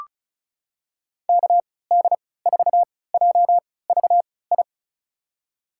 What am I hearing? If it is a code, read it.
Morse code, heard as a keyed tone, about 34 wpm, spaced as for 20 wpm, 700 Hz. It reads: KD4JVI
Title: USA Call Signs - 3x Character Spacing - 35wpm